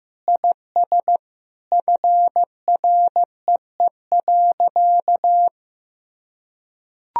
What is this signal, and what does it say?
Morse code, heard as a keyed tone, about 15 wpm, 700 Hz. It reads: IS FREE.
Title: Sentences from Top 1000 Words - Review Part 1 - 15wpm